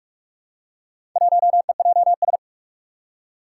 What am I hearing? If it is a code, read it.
Morse code, heard as a keyed tone, about 44 wpm, 700 Hz. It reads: 1EJS
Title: Four Challenging Letter-Number Combinations 45wpm